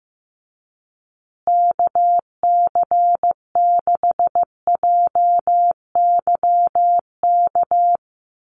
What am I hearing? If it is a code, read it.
Morse code, heard as a keyed tone, about 15 wpm, 700 Hz. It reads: KC6JYK
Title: USA Call Signs Repeated 3x 15wpm